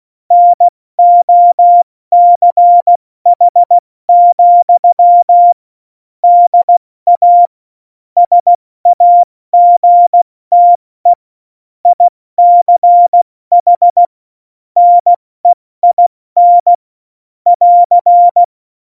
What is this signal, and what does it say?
Morse code, heard as a keyed tone, about 16 wpm, 700 Hz. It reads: NOCH, DA SAGTE ICH NEIN <AR>
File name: Morsekurs / Witze / Witz 21 Juli 2020, 70 F80